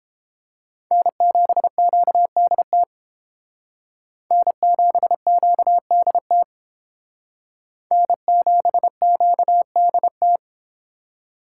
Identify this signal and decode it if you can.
Morse code, heard as a keyed tone, about 30 wpm, 700 Hz. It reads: N7QDT N7QDT N7QDT